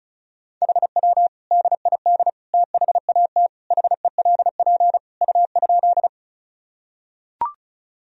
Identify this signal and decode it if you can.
Morse code, heard as a keyed tone, about 34 wpm, 700 Hz. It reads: HW DID THAT HELP U?
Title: Sentences from Top 200 Words and Abbreviations 35wpm